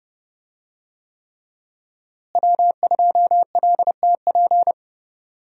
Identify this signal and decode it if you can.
Morse code, heard as a keyed tone, about 30 wpm, 700 Hz. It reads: W2LTP
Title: USA Call Signs - 5 Characters - Repeated 3x - 30wpm